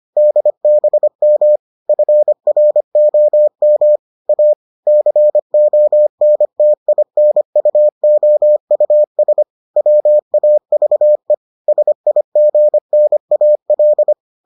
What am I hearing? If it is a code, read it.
Morse code, heard as a keyed tone, about 25 wpm, 600 Hz. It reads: DBM FROM A CONTINUOUS WAVE SIGNAL